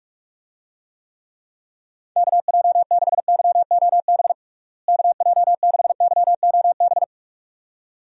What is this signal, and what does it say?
Morse code, heard as a keyed tone, about 41 wpm, 700 Hz. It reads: KJ6YOB KJ6YOB
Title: USA Call Signs - 6 Characters - Repeated 2x - 45wpm